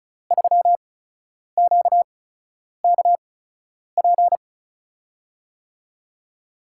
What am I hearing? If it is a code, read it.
Morse code, heard as a keyed tone, about 34 wpm, spaced as for 10 wpm, 700 Hz. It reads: 3QKP
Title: USA Call Signs - 8x Character Spacing - 35wpm